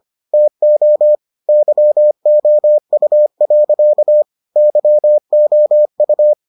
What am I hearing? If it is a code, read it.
Morse code, heard as a keyed tone, about 25 wpm, 600 Hz. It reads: TO YOU. YOU